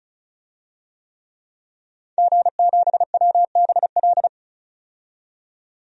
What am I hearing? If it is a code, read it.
Morse code, heard as a keyed tone, about 35 wpm, 700 Hz. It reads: G7WBL
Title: DXCC-100 Call Signs - 5 Characters - 35wpm